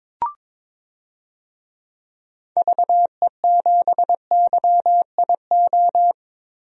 Morse code, heard as a keyed tone, about 22 wpm, 700 Hz. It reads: VE7YIO
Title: DXCC-100 Call Signs - 6 Characters - 22wpm